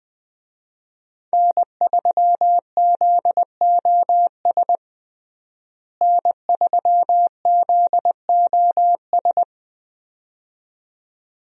Morse code, heard as a keyed tone, about 21 wpm, 700 Hz. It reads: N3ZOS N3ZOS
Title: USA Call Signs - 5 Characters - Repeated 2x - 20wpm